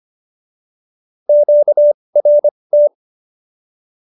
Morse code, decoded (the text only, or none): QRT